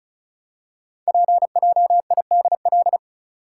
Morse code, heard as a keyed tone, about 36 wpm, 700 Hz. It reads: PJIDL
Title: Code Groups - 5 Letters - 35wpm